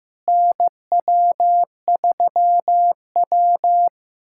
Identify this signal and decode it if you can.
Morse code, heard as a keyed tone, about 15 wpm, 700 Hz. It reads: NW3W
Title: USA Callsign DE Callsign - 4 Characters - 15wpm